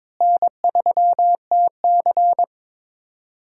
Morse code, heard as a keyed tone, about 22 wpm, 700 Hz. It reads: N3TC
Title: USA Call Signs - 4 Characters - Repeated 3x - 22wpm